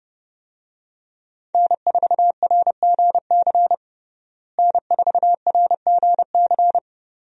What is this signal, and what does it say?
Morse code, heard as a keyed tone, about 30 wpm, 700 Hz. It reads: N4RGC N4RGC
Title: USA Call Signs - 5 Characters - Repeated 3x - 30wpm